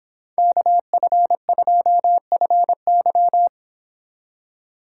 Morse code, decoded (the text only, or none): KF2FY